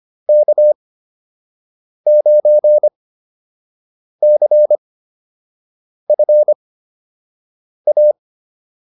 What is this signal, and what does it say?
Morse code, heard as a keyed tone, about 25 wpm, spaced as for 6 wpm, 600 Hz. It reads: K9CFA